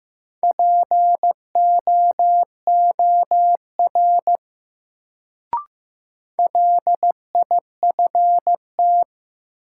Morse code, heard as a keyed tone, about 15 wpm, 700 Hz. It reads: POOR LIFT
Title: Sentences from Top 1000 Words - New Word Drill - 15wpm